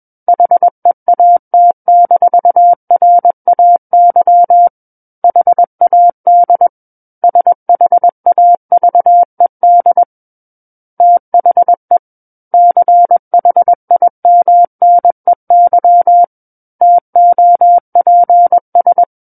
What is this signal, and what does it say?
Morse code, heard as a keyed tone, about 21 wpm, 700 Hz. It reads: HEAT-RAY HAD SHAVED THE CHIMNEY TOPS
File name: Book One / 12-Chapter 12 - 21 WPM